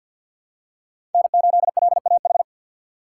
Morse code, heard as a keyed tone, about 46 wpm, 700 Hz. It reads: N8LAH
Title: USA Call Signs - 5 Characters - 3x Character Spacing - 50wpm